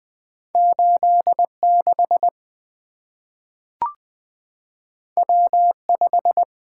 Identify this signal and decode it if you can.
Morse code, heard as a keyed tone, about 20 wpm, 700 Hz. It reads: 86 W5
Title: Two Letter-Number Combinations - 20wpm